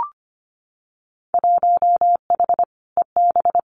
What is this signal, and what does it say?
Morse code, heard as a keyed tone, about 25 wpm, 700 Hz. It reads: 1H EB
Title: Winter Field Day Exchanges 25wpm